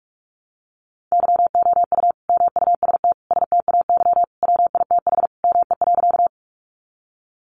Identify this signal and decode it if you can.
Morse code, heard as a keyed tone, about 44 wpm, 700 Hz. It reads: YOU MUST STAY WITH ME.
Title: Sentences from Top 500 Words 45wpm